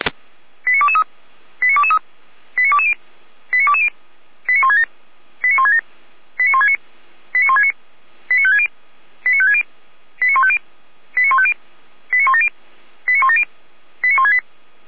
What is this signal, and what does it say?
Five-tone selective call (ZVEI), heard as quick runs of five tones, tones of about 70 ms. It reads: 89223 89223 89220 89220 89167 89167 89169 89169 89560 89560 89150 89150 89170 89170 89168